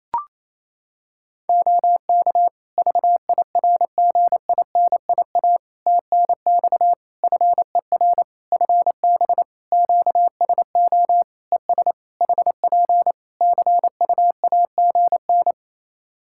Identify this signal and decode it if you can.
Morse code, heard as a keyed tone, about 28 wpm, 700 Hz. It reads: OK VIRGINIA TNX FER FB QSO ES HP CUAGN
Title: Practice QSO - Exchange 4 - Ending - 28wpm